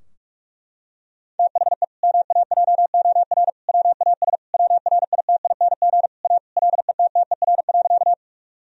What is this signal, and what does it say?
Morse code, heard as a keyed tone, about 46 wpm, 700 Hz. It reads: THE MAJOR WAS WRITING A LETTER.
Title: Sentences from Top 1000 Words - Review Part 4 - 45wpm